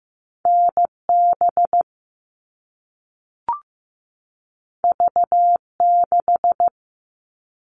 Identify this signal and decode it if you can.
Morse code, heard as a keyed tone, about 15 wpm, 700 Hz. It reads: NB V6
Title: All Cut Numbers - 2-Digit 15wpm